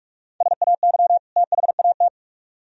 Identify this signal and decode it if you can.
Morse code, heard as a keyed tone, about 44 wpm, 700 Hz. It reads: SAY THAT